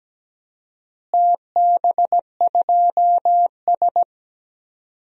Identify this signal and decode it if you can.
Morse code, heard as a keyed tone, about 17 wpm, 700 Hz. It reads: TB2S
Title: DXCC-100 Call Signs - 4 Characters - Repeated 3x - 17wpm